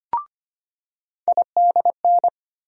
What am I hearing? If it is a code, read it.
Morse code, heard as a keyed tone, about 25 wpm, 700 Hz. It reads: IDN
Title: USA Call Sign Postfixes - 3 Letters 25wpm